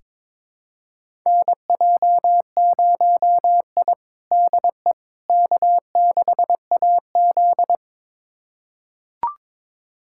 Morse code, heard as a keyed tone, about 22 wpm, 700 Hz. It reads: NJ0I DE K6AZ
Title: USA Callsign DE Callsign - 4 Characters - 22wpm